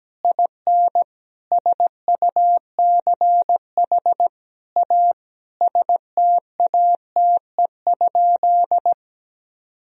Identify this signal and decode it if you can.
Morse code, heard as a keyed tone, about 17 wpm, 700 Hz. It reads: IN SUCH A STATE?
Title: Sentences from Top 1000 Words - Review Part 2 - 17wpm